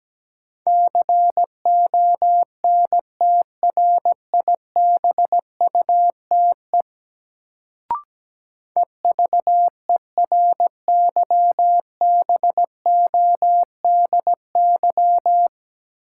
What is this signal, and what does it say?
Morse code, heard as a keyed tone, about 17 wpm, 700 Hz. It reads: CONTRIBUTE EVERYBODY
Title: Sentences from Top 1400 Words - New Word Drill - 17wpm